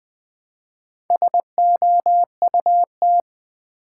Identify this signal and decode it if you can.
Morse code, heard as a keyed tone, about 21 wpm, 700 Hz. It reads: SOUT
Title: Build A Word - 2 Letters At A Time - 8-Letter Words - 20wpm